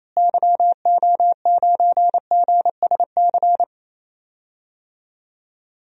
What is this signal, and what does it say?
Morse code, heard as a keyed tone, about 28 wpm, 700 Hz. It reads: YO9GSC
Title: DXCC-100 Call Signs - 6 Characters - Repeated 3x - 28wpm